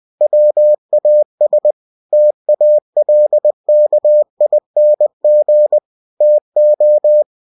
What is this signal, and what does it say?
Morse code, heard as a keyed tone, about 20 wpm, 600 Hz. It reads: WAS TALKING TO